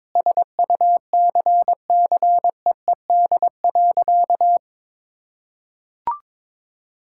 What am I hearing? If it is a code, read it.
Morse code, heard as a keyed tone, about 22 wpm, 700 Hz. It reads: SUCCEED.
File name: practice / 22wpm / Sentences with 8-Word Phrases - 22wpm